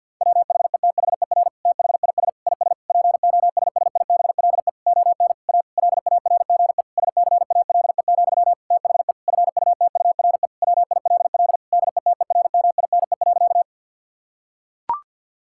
Morse code, heard as a keyed tone, about 50 wpm, 700 Hz. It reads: WHETHER THIS IS POSSIBLE ON A LARGE SCALE, THE FUTURE WILL DETERMINE.
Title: Sentences from Top 1000 Words - Review Part 5 - 50wpm